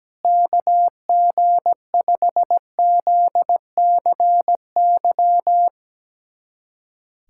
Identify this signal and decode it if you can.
Morse code, heard as a keyed tone, about 17 wpm, 700 Hz. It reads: KG5ZCY
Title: USA Callsign DE Callsign - 6 Characters - 17wpm